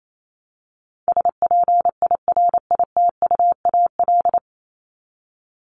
Morse code, heard as a keyed tone, about 28 wpm, 700 Hz. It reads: SPIRITUAL
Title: Top 1501-1750 Words 28wpm